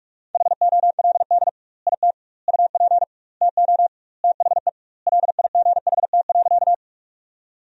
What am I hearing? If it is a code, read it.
Morse code, heard as a keyed tone, about 46 wpm, 700 Hz. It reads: HOLD IT UP TO THE LIGHT.
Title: 2x Word-Spacing - Sentences from Top 400 Words - 45wpm